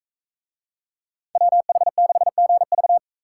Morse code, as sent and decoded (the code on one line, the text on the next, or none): .-- .... -.... --. ...-
WH6GV